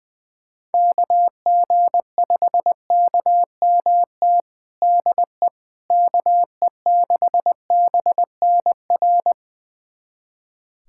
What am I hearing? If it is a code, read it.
Morse code, heard as a keyed tone, about 20 wpm, 700 Hz. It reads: KG5KMT DE KE6BNR
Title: USA Callsign DE Callsign - 6 Characters - 20wpm